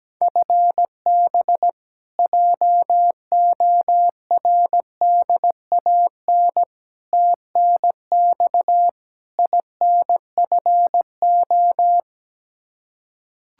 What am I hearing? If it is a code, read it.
Morse code, heard as a keyed tone, about 17 wpm, 700 Hz. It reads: FB JORDAN TNX INFO
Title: Practice QSO - Exchange 3 - More Info - 17wpm